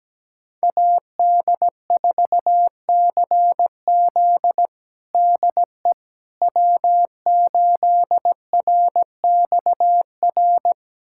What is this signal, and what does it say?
Morse code, heard as a keyed tone, about 17 wpm, 700 Hz. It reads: AD4CZ DE W8RXR
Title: USA Callsign DE Callsign - 5 Characters - 17wpm